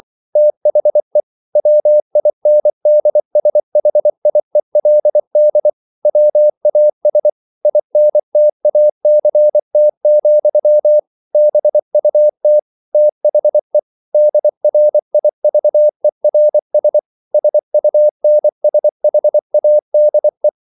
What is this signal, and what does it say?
Morse code, heard as a keyed tone, about 24 wpm, 600 Hz. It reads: THE WINDSHIELD WAS INTACT, BUT THE DRIVERS SUNSHADE